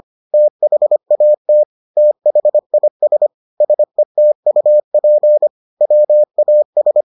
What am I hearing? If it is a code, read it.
Morse code, heard as a keyed tone, about 25 wpm, 600 Hz. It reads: THAT THIS SETUP WAS